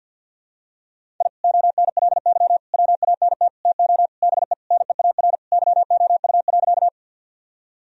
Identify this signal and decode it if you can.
Morse code, heard as a keyed tone, about 49 wpm, 700 Hz. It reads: I ONLY WANT TO BE NEAR YOU.
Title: Sentences from Top 1000 Words - Review Part 1 - 50wpm